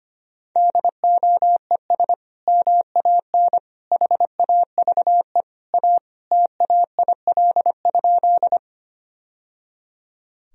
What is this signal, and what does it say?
Morse code, heard as a keyed tone, about 25 wpm, 700 Hz. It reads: DOES MAN HAVE A TAIL?
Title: Sentences from Top 1000 Words - Review Part 3 - 25wpm